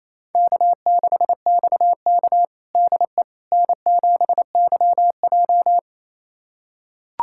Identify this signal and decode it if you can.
Morse code, heard as a keyed tone, about 28 wpm, 700 Hz. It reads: K6XK DE N7YJ
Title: USA Callsign DE Callsign - 4 Characters - 28wpm